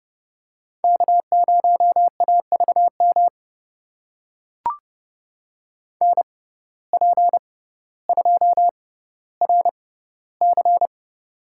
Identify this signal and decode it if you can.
Morse code, heard as a keyed tone, about 30 wpm, spaced as for 15 wpm, 700 Hz. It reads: K0AVM NP2RC
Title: USA Call Signs - 5 Characters - 6x Character Spacing - 30wpm